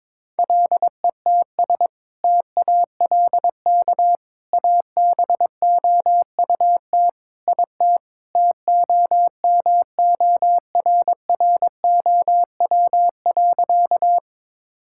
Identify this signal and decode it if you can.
Morse code, heard as a keyed tone, about 22 wpm, 700 Hz. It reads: LETS TALK ABOUT IT TOMORROW.
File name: practice / 22wpm / Sentences with 2-Word Phrases - 22wpm